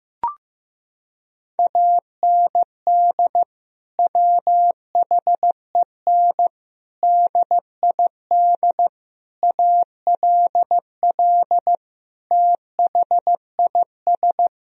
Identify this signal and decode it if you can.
Morse code, heard as a keyed tone, about 16 wpm, 700 Hz. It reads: AND WHEN DID ALL THIS
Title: Sentences from Top 1000 Words - Review Part 4 - 15wpm